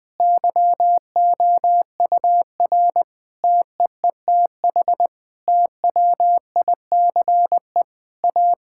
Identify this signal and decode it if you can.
Morse code, heard as a keyed tone, about 20 wpm, 700 Hz. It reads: YOUR TEETH TWICE A
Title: Difficult Sentences with Hints - 20wpm